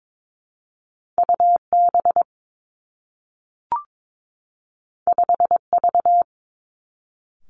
Morse code, heard as a keyed tone, about 22 wpm, 700 Hz. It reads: U B 5 V
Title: Two Challenging Letter-Number Combinations 22wpm